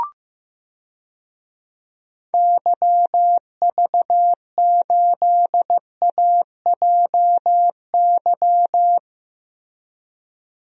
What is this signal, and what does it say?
Morse code, heard as a keyed tone, about 15 wpm, 700 Hz. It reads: YV8AJY
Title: DXCC-100 Call Signs - 6 Characters - Repeated 3x - 15wpm